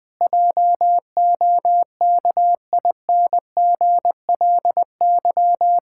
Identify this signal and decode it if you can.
Morse code, heard as a keyed tone, about 21 wpm, 700 Hz. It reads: JOKINGLY
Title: Pangram Challenge v2 - 20wpm